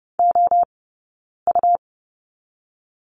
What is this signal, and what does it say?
Morse code, heard as a keyed tone, about 30 wpm, 700 Hz. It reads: O U
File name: Lesson 18d - Letter m - Two Characters 30wpm